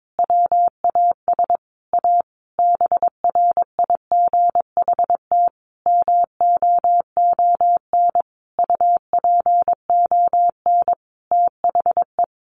Morse code, heard as a keyed tone, about 22 wpm, 700 Hz. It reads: WAS A BRIGHT MOON UPON THE
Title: Sentences from Top 1500 Words 22wpm